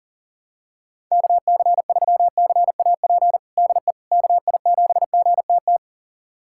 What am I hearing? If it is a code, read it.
Morse code, heard as a keyed tone, about 40 wpm, 700 Hz. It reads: KC3CAP DE KI7GTT
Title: USA Callsign DE Callsign 40wpm